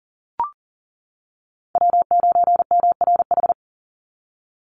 Morse code, heard as a keyed tone, about 40 wpm, 700 Hz. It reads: W9MRH
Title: NA SOTA Chaser Callsigns 40wpm